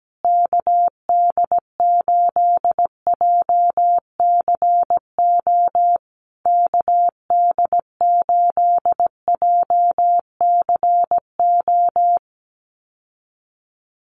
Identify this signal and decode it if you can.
Morse code, heard as a keyed tone, about 17 wpm, 700 Hz. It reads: KD8JCO KD8JCO
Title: USA Call Signs Repeated 3x 17wpm